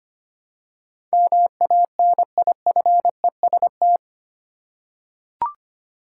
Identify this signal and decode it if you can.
Morse code, heard as a keyed tone, about 25 wpm, 700 Hz. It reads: MANIFEST
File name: 8 Letter Words - First 3 Letters Given - 25wpm